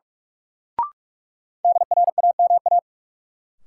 Morse code, heard as a keyed tone, about 42 wpm, 700 Hz. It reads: DRAMA
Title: Sentences from Top 1500 Words - New Word Drill Alt - 45wpm